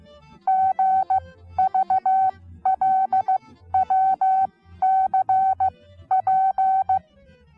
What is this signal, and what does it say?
Morse code, heard as a keyed tone, about 15 wpm, spaced as for 12 wpm, 760 Hz. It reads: GVLWCP
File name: Onlinelehrgang / morsen / DJ4UF Morsen19